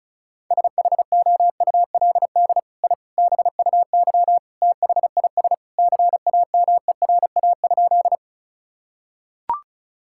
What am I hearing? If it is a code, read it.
Morse code, heard as a keyed tone, about 34 wpm, 690 Hz. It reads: SHOULD I BUY THIS CAMERA?
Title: Sentences from Top 1000 Words - Review Part 5 - 35wpm